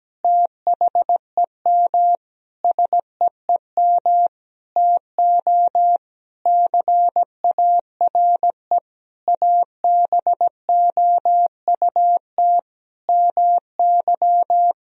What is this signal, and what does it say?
Morse code, heard as a keyed tone, about 17 wpm, 700 Hz. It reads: THEM SEEM TO CARE ABOUT MY
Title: Sentences from Top 1000 Words - Review Part 3 - 17wpm